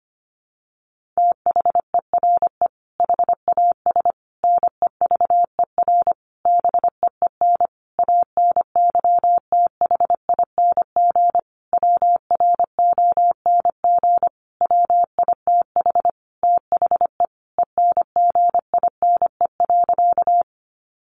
Morse code, heard as a keyed tone, about 25 wpm, 700 Hz. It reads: THERE HAS NEVER BEEN ANYTHING WRONG WITH THE ENGINE.
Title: Sentences from Top 1500 Words 25wpm